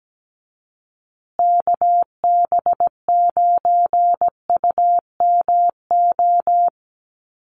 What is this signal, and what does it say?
Morse code, heard as a keyed tone, about 17 wpm, 700 Hz. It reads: KB9UMO
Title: USA Call Signs Repeated 2x 17wpm